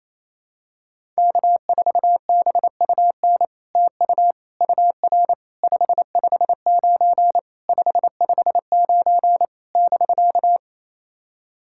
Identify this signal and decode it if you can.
Morse code, heard as a keyed tone, about 27 wpm, 700 Hz. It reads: K4BUN TU UR 559 559 <BK>
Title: SOTA QSO Practice 28wpm